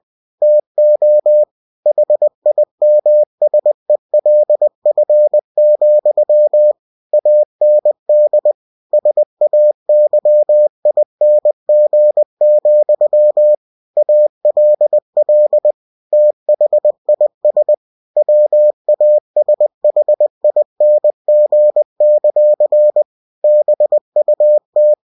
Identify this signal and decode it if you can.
Morse code, heard as a keyed tone, about 20 wpm, 600 Hz. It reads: TO HIMSELF, AND SAYING, ALL THIS WASHING; BUT
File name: winnie-20wpm